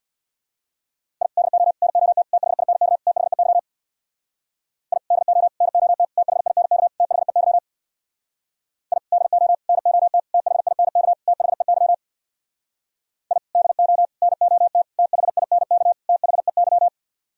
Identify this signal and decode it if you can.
Morse code, heard as a keyed tone, about 61 wpm, 700 Hz. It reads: I DO NOT THINK THEY I DO NOT THINK THEY I DO NOT THINK THEY I DO NOT THINK THEY
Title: Speed-Racing - Sets of 5 Words 50wpm